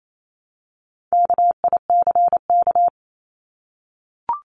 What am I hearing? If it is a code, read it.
Morse code, heard as a keyed tone, about 28 wpm, 700 Hz. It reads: KICK
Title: Top 1251-1500 Words 28wpm